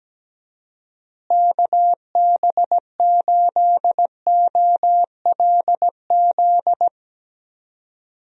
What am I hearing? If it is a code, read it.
Morse code, heard as a keyed tone, about 17 wpm, 700 Hz. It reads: KB8OLZ
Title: USA Call Signs - 6 Characters - Repeated 3x - 17wpm